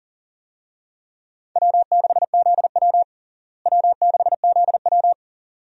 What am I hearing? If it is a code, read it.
Morse code, heard as a keyed tone, about 37 wpm, 700 Hz. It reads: W6ZW W6ZW
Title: USA Call Signs - 4 Characters - Repeated 2x - 40wpm